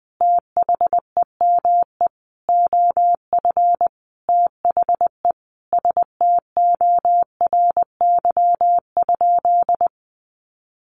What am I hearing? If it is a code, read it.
Morse code, heard as a keyed tone, about 20 wpm, 700 Hz. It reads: THEME OF THE STORY?
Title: Sentences from Top 1100 Words 20wpm